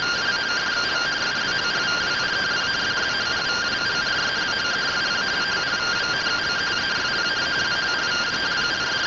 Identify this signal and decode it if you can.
Two-tone teleprinter signal, about 170 Hz shift.